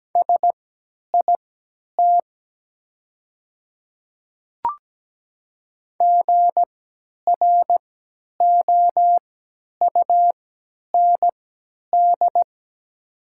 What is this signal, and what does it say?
Morse code, heard as a keyed tone, about 17 wpm, spaced as for 10 wpm, 700 Hz. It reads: SIT GROUND
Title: 3x Character Spacing - Top 500 Words - 17wpm